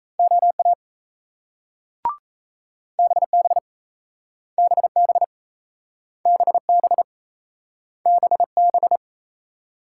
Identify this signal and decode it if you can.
Morse code, heard as a keyed tone, about 38 wpm, 700 Hz. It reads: OA BB BB BB BB